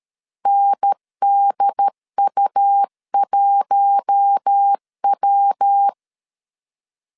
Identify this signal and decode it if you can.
Morse code, heard as a keyed tone, about 13 wpm, 800 Hz. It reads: NDU1W
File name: CW / DET / 1 6 1500C S A 12wpm 3440